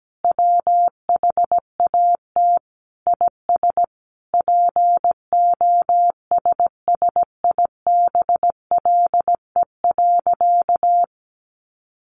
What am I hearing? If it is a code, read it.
Morse code, heard as a keyed tone, about 17 wpm, 700 Hz. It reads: WHAT IS POSSIBLE.
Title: Sentences from Top 900 Words 17wpm